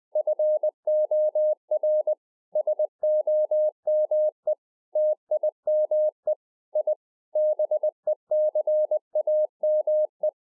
Morse code, heard as a keyed tone, about 20 wpm, 610 Hz. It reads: FOR SOME TIME I BECAME